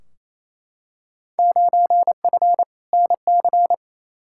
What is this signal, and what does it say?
Morse code, heard as a keyed tone, about 28 wpm, 700 Hz. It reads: 9F NC
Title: Field Day Exchanges Encore - 28wpm